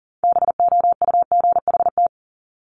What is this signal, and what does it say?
Morse code, heard as a keyed tone, about 41 wpm, 700 Hz. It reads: BOUGHT
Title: Multi-letter Phonemes 40wpm